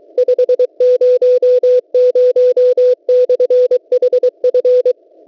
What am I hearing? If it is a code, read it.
Morse code, heard as a keyed tone, about 23 wpm, 500 Hz. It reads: 500/HF
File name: WebSDR records / w7pua-2.ddns.net 8642.00 cw